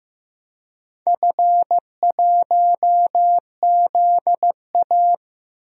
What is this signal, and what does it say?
Morse code, heard as a keyed tone, about 15 wpm, 700 Hz. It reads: F1ZA
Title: DXCC-100 Call Signs - 4 Characters - Repeated 3x - 15wpm